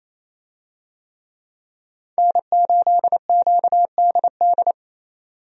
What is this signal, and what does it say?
Morse code, heard as a keyed tone, about 28 wpm, 700 Hz. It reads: N8QDD
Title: USA Call Signs - 5 Characters - 28wpm